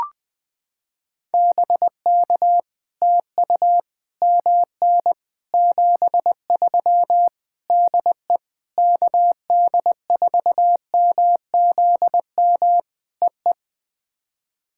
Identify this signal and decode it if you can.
Morse code, heard as a keyed tone, about 20 wpm, 700 Hz. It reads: BK TU MN 73 DE KD4MZM EE
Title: POTA QSO Practice - 20wpm